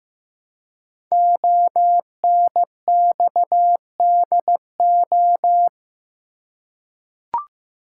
Morse code, heard as a keyed tone, about 15 wpm, 700 Hz. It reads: ONXDO
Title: Code Groups - 5 Letters - 15wpm